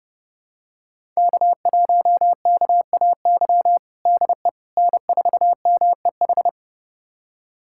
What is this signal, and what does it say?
Morse code, heard as a keyed tone, about 30 wpm, 700 Hz. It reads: K1KAY DE N4MEH
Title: USA Callsign DE Callsign - 5 Characters - 30wpm